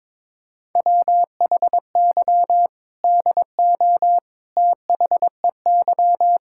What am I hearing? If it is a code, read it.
Morse code, heard as a keyed tone, about 22 wpm, 700 Hz. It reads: WHY DO THEY
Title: Sentences from Top 1600 Words 22wpm